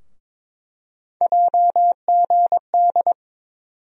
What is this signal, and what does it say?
Morse code, heard as a keyed tone, about 22 wpm, 700 Hz. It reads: JGD
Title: USA Call Sign Postfixes - 3 Letters 22wpm